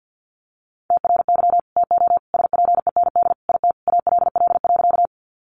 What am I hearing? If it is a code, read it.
Morse code, heard as a keyed tone, about 50 wpm, 700 Hz. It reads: TRY TO SPEND IT ALL.
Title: Sentences from Top 800 Words 50wpm